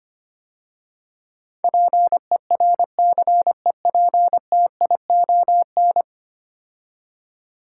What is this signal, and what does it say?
Morse code, heard as a keyed tone, about 25 wpm, 700 Hz. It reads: PERCEPTION
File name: 10 Letter Words - First 3 Letters Given - 25wpm